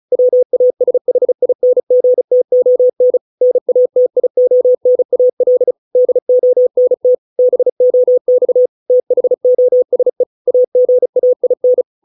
Morse code, heard as a keyed tone, about 34 wpm, 500 Hz. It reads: WASHINGTON NATIONAL DONT BOX THOSE AGAIN